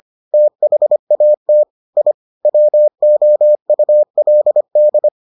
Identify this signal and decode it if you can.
Morse code, heard as a keyed tone, about 25 wpm, 600 Hz. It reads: THAT I WOULD